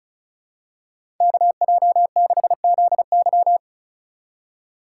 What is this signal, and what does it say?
Morse code, heard as a keyed tone, about 36 wpm, 700 Hz. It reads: KJ6ZY
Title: USA Call Signs - 5 Characters - 4x Character Spacing - 35wpm